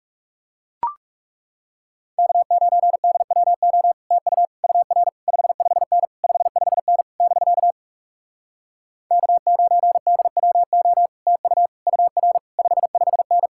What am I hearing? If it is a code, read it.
Morse code, heard as a keyed tone, about 37 wpm, 700 Hz. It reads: K9DWO TU UR 55N 55N <BK> K9DWO TU UR 55N
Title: Speed-Racing Common QSO Phrases Encore - 30wpm